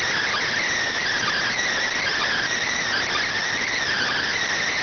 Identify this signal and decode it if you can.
Emergency Alert System (SAME) tones, heard as harsh buzzing bursts. It is heard from 1.1 to 4.4 s.